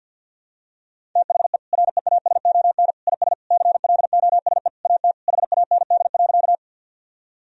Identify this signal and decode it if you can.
Morse code, heard as a keyed tone, about 53 wpm, 700 Hz. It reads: THE REASON IS CLOSE AT HAND.
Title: Sentences from Top 1000 Words - Review Part 1 - 50wpm